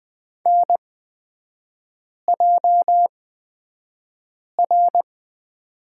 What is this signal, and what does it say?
Morse code, heard as a keyed tone, about 20 wpm, spaced as for 5 wpm, 700 Hz. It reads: NJR
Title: 243 Lesson 39e - Letter z - Three Characters 20wpm